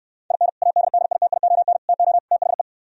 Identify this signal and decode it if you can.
Morse code, heard as a keyed tone, about 66 wpm, 700 Hz. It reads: IN ADDITION TO THE